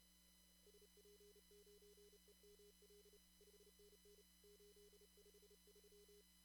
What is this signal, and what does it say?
Morse code, heard as a keyed tone, about 31 wpm, 410 Hz. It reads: SP9EML 5NN 853
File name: CW / kt5j wpx cw-op-k5pi